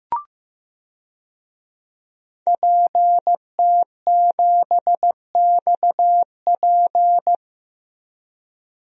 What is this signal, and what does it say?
Morse code, heard as a keyed tone, about 15 wpm, 700 Hz. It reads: PT7XP
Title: DXCC-100 Call Signs - 5 Characters - 15wpm